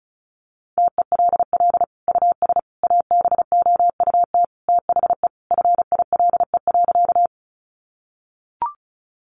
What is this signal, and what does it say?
Morse code, heard as a keyed tone, about 34 wpm, 700 Hz. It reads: TELL US ABOUT THE FILE.
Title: Sentences from Top 1200 Words 35wpm